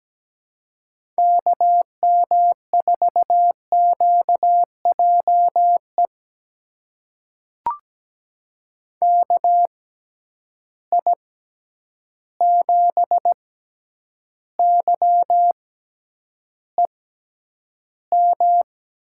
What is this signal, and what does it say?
Morse code, heard as a keyed tone, about 17 wpm, spaced as for 9 wpm, 700 Hz. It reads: KM4QJE KI7YEM
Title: USA Call Signs - 6 Characters - 6x Character Spacing - 17wpm